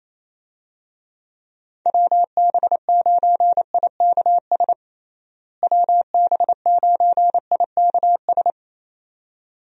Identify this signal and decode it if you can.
Morse code, heard as a keyed tone, about 27 wpm, 700 Hz. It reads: WB9IKS WB9IKS
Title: USA Call Signs - 6 Characters - Repeated 2x - 28wpm